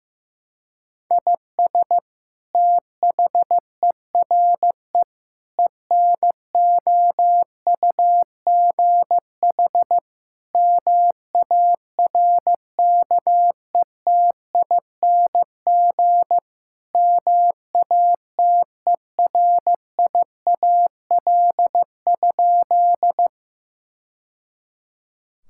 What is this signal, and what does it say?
Morse code, heard as a keyed tone, about 16 wpm, 700 Hz. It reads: IS THERE ENOUGH MARKETING MATERIAL?
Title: Sentences from Top 1000 Words - Review Part 5 - 15wpm